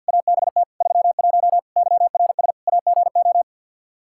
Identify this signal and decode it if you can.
Morse code, heard as a keyed tone, about 52 wpm, 700 Hz. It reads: ABT 31 YRS AGO
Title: Practice QSO - Exchange 3 - More Info - 50wpm